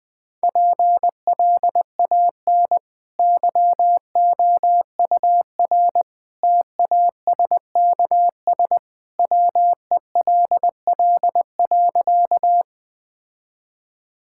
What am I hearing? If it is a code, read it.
Morse code, heard as a keyed tone, about 20 wpm, 700 Hz. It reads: PLAN YOUR TASKS WELL.
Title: Easy Sentences with Hints - 20wpm